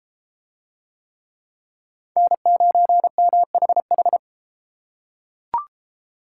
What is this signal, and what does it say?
Morse code, heard as a keyed tone, about 30 wpm, 700 Hz. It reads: N9MHH